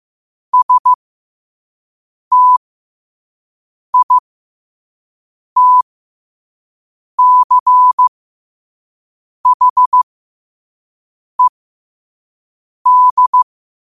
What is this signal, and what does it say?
Morse code, heard as a keyed tone, about 15 wpm, spaced as for 5 wpm, 1000 Hz. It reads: STITCHED
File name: qotdfiles / 5wpm